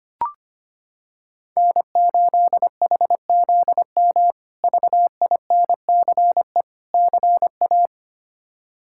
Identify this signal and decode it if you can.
Morse code, heard as a keyed tone, about 25 wpm, 700 Hz. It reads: N8HZM VINCE CA
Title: K1USN SST QSO Practice - 25wpm